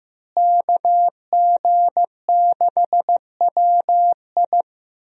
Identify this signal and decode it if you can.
Morse code, heard as a keyed tone, about 15 wpm, 700 Hz. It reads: KG6WI
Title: USA Call Signs - 5 Characters - Repeated 3x - 15wpm